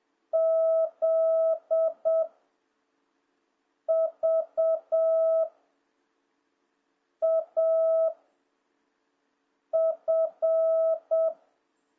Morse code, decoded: Z V A F

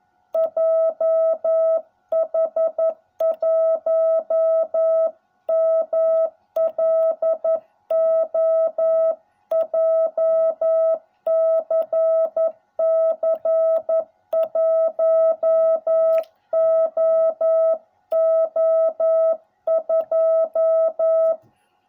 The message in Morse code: JH1MLO JCC 1OO2

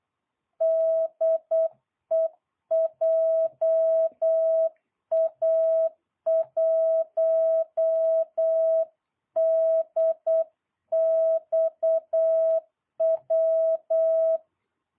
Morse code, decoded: DE JA1DXW

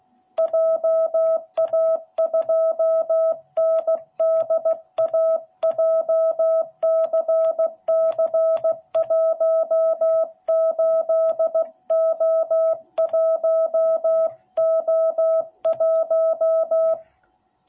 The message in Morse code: JA2NDA JCC18O1O1

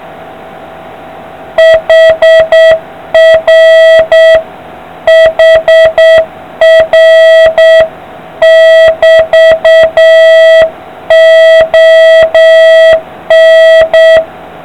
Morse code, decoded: HR HR <BT>ON

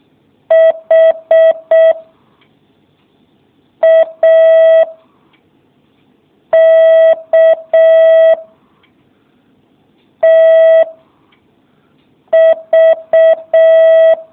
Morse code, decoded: H A K T V

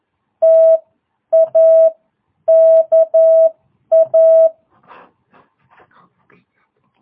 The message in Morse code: TA KA

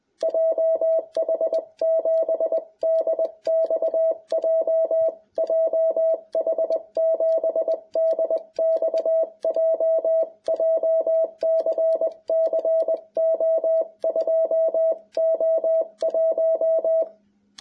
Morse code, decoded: JH7DXJ JH7DXJ JCC O2O1

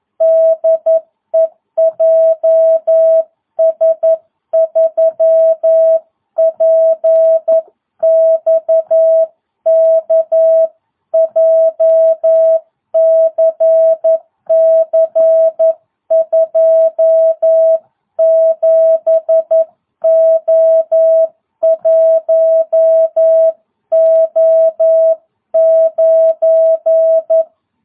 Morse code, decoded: DE JS3PXK JCC 27O1O9

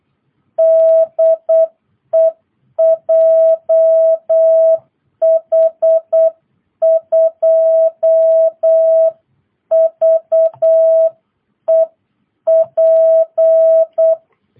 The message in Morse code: DE JH2VEP

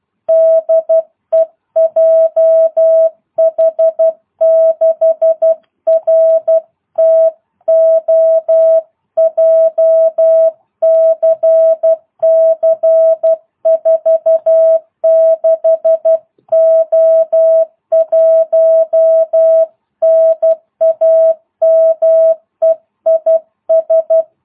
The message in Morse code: DE JH6RTO JCC 46O1 NAME IS

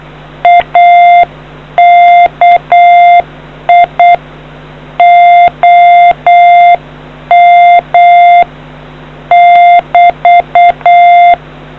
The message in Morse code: AKI OM <BT>

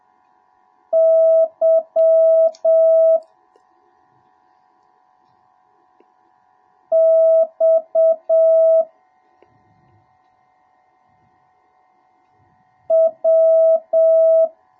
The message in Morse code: Y X W

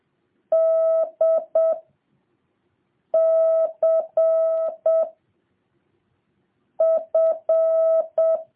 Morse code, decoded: D C F

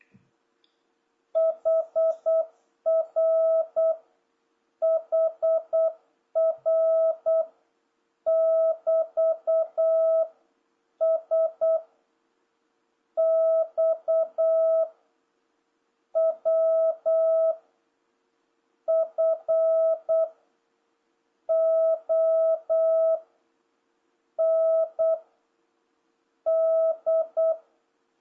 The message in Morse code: HR HR <BT> S X W F O N D

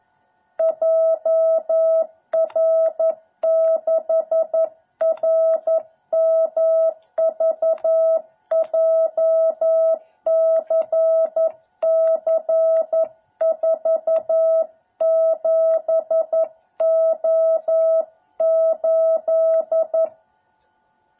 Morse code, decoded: JR6RMV JCC47O8

